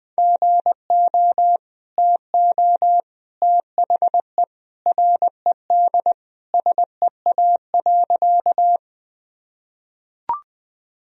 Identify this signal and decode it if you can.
Morse code, heard as a keyed tone, about 20 wpm, 700 Hz. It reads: GO TO THE RED SEA.
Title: Sentences from Top 1000 Words - Review Part 2 - 20wpm